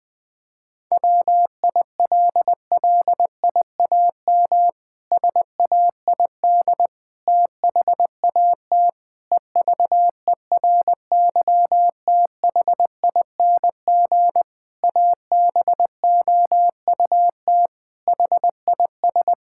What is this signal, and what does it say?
Morse code, heard as a keyed tone, about 20 wpm, 700 Hz. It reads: WILLIAM SAID THAT EVERYTHING ABOUT HIS
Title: Pangram Challenge v2 - 20wpm